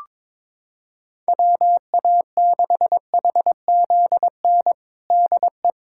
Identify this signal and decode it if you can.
Morse code, heard as a keyed tone, about 22 wpm, 700 Hz. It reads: WA6HZN DE
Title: USA Callsign DE Callsign - 6 Characters - 22wpm